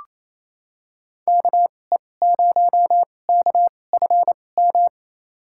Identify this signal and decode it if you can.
Morse code, heard as a keyed tone, about 28 wpm, spaced as for 20 wpm, 700 Hz. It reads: KE0KFM
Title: USA Call Signs - 6 Characters - 2x Character Spacing - 28wpm